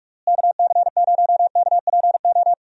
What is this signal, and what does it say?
Morse code, heard as a keyed tone, about 41 wpm, 700 Hz. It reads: KC0KPO
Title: USA Call Signs - 6 Characters - Repeated 3x - 45wpm